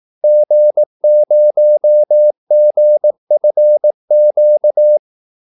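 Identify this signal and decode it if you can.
Morse code, heard as a keyed tone, about 18 wpm, 600 Hz. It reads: G0GFQ
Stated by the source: Morse call-sign practice at 18 wpm.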